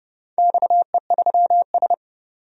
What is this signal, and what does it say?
Morse code, heard as a keyed tone, about 30 wpm, 700 Hz. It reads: XE3S
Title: DXCC-100 Call Signs - 4 Characters - 30wpm